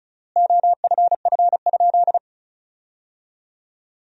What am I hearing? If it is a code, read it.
Morse code, heard as a keyed tone, about 34 wpm, 700 Hz. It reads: OFF?
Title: 3x Word-Spacing - Sentences from Top 500 Words - 35wpm